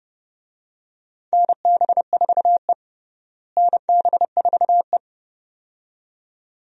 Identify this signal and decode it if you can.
Morse code, heard as a keyed tone, about 30 wpm, 700 Hz. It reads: NB4E NB4E
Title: USA Call Signs - 4 Characters - Repeated 2x - 30wpm